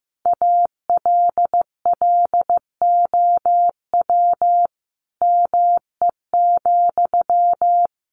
Morse code, heard as a keyed tone, about 15 wpm, 700 Hz. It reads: ALLOW ME,
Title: Sentences from Top 700 Words 15wpm